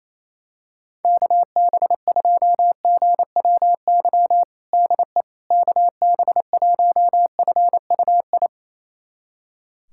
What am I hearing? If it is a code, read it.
Morse code, heard as a keyed tone, about 28 wpm, 700 Hz. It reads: KB2GWY DE KB1FUI
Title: USA Callsign DE Callsign - 6 Characters - 28wpm